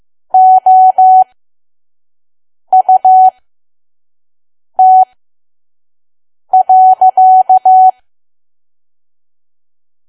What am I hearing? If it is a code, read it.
Morse code, heard as a keyed tone, about 15 wpm, spaced as for 5 wpm, 750 Hz. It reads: OUT.
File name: Archive / 5 WPM files / PM